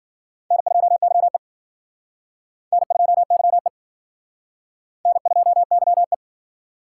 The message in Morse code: N2YE N2YE N2YE

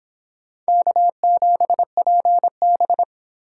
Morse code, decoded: K7PB

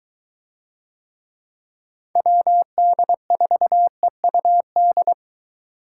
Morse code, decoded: WD4EUD

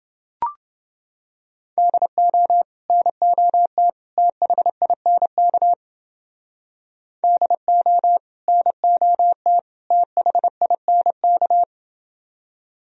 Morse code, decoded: DO NOT THINK DO NOT THINK

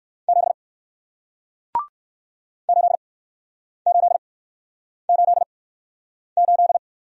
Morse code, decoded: 7 8888

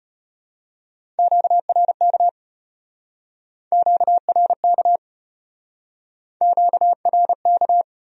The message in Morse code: QRK QRK QRK

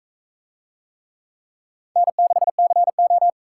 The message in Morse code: N6CO

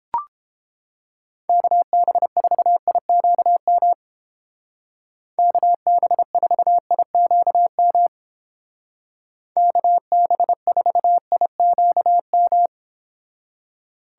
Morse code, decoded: KB4IQM KB4IQM KB4IQM

